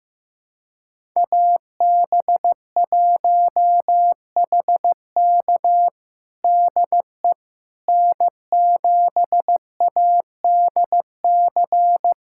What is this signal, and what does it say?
Morse code, heard as a keyed tone, about 15 wpm, 700 Hz. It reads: AB1HK DE N7ADC
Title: USA Callsign DE Callsign - 5 Characters - 15wpm